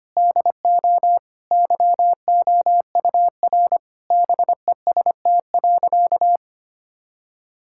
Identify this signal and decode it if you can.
Morse code, heard as a keyed tone, about 25 wpm, 700 Hz. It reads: DO YOUR BEST.
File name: practice / 25wpm / Sentences with 2-Word Phrases - 25wpm